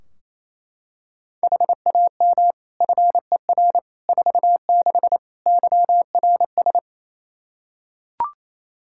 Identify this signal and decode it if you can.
Morse code, heard as a keyed tone, about 27 wpm, 700 Hz. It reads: HAM FER 46 YRS
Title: Practice QSO - Exchange 3 - More Info - 28wpm